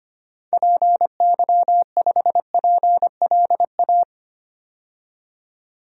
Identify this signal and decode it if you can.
Morse code, heard as a keyed tone, about 25 wpm, 700 Hz. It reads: PY5PLA
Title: DXCC-100 Call Signs - 6 Characters - 25wpm